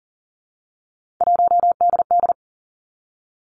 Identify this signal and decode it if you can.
Morse code, heard as a keyed tone, about 39 wpm, 700 Hz. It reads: 1DD
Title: Three Challenging Letter-Number Combinations 40wpm